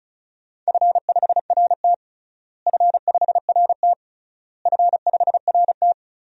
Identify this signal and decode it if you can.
Morse code, heard as a keyed tone, about 36 wpm, 700 Hz. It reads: F5RT F5RT F5RT
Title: DXCC-100 Call Signs - 4 Characters - Repeated 3x - 35wpm